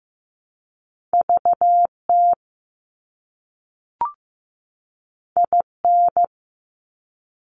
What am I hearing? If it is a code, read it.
Morse code, heard as a keyed tone, about 15 wpm, 700 Hz. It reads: VT IN
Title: US States 15wpm